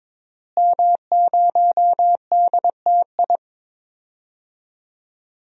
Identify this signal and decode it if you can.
Morse code, heard as a keyed tone, about 22 wpm, 700 Hz. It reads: M0DTI
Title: DXCC-100 Call Signs - 5 Characters - Repeated 3x - 22wpm